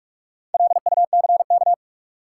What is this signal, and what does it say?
Morse code, heard as a keyed tone, about 47 wpm, 700 Hz. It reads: LUCK
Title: Sentences from Top 1500 Words - New Word Drill - 45wpm